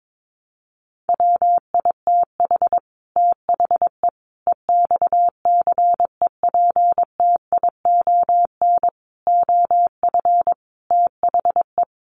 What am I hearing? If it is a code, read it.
Morse code, heard as a keyed tone, about 22 wpm, 700 Hz. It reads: WITH THE EXCEPTION OF THE
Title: Sets of 5 Words Encore 22wpm